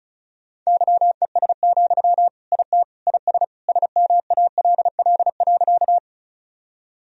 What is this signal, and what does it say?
Morse code, heard as a keyed tone, about 34 wpm, 700 Hz. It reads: YES, IT IS SMALL.
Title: Sentences from Top 1000 Words - Review Part 1 - 35wpm